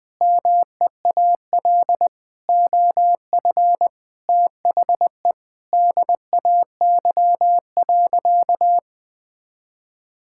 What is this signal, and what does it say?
Morse code, heard as a keyed tone, about 21 wpm, 700 Hz. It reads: MEAL OF THE DAY.
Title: Sentences with 9-Word Phrases - Spoken Beginning - 20wpm